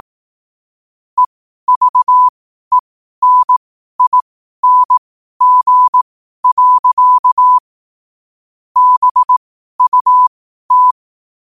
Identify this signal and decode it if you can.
Morse code, heard as a keyed tone, about 18 wpm, spaced as for 12 wpm, 1000 Hz. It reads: EVENING. BUT
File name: qotdfiles / 12wpm